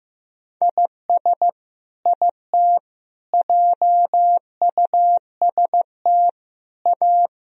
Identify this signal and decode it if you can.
Morse code, heard as a keyed tone, about 15 wpm, 700 Hz. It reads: IS IT JUST A
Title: Sentences from Top 1000 Words - Review Part 1 - 15wpm